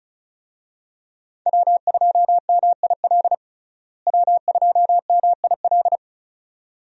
Morse code, heard as a keyed tone, about 36 wpm, 700 Hz. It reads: W2MIL W2MIL
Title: USA Call Signs - 5 Characters - Repeated 2x - 35wpm